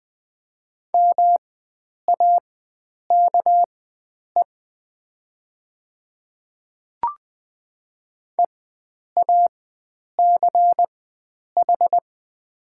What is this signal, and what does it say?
Morse code, heard as a keyed tone, about 20 wpm, spaced as for 9 wpm, 700 Hz. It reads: MAKE EACH
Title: 4x Character Spacing - Top 300 Words - 20wpm